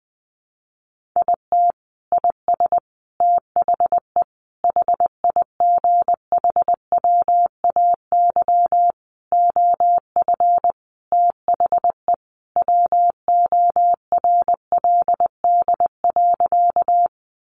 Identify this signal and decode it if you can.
Morse code, heard as a keyed tone, about 20 wpm, 700 Hz. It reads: IT IS THE HIGHWAY OF THE WORLD.
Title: Sentences from Top 1400 Words 20wpm